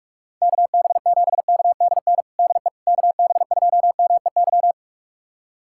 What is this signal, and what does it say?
Morse code, heard as a keyed tone, about 46 wpm, 700 Hz. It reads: KB7KDN DE KB2MEY
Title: USA Callsign DE Callsign - 6 Characters - 45wpm